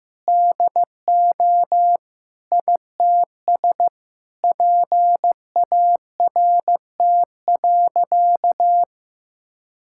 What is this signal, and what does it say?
Morse code, heard as a keyed tone, about 16 wpm, 700 Hz. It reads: DO ITS PART.
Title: Sentences from Top 1000 Words - Review Part 5 - 15wpm